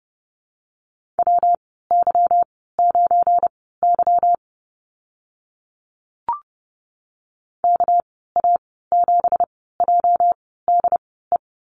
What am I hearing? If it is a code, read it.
Morse code, heard as a keyed tone, about 30 wpm, spaced as for 17 wpm, 700 Hz. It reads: WY9Y KA7JDE
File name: Lesson 37g - Letter x - Call Signs 30wpm